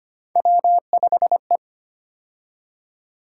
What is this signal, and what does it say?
Morse code, heard as a keyed tone, about 25 wpm, 700 Hz. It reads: W5E
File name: Three Letter-Number Combinations - 25wpm